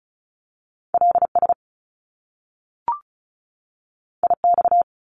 Morse code, heard as a keyed tone, about 36 wpm, 700 Hz. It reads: LS IX
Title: Top Two Letter Combinations in Words 35wpm